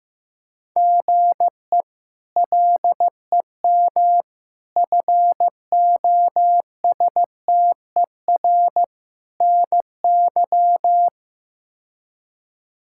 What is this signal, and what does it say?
Morse code, heard as a keyed tone, about 15 wpm, 700 Hz. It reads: GE LEM FOSTER NY
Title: K1USN SST QSO Practice - 15wpm